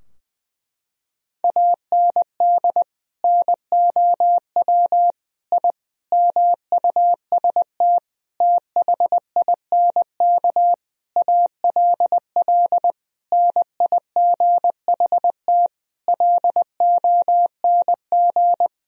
Morse code, heard as a keyed tone, about 20 wpm, 700 Hz. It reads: AND NOW I MUST THINK ALL NIGHT LONG
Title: Sentences from Top 1000 Words - Review Part 3 - 20wpm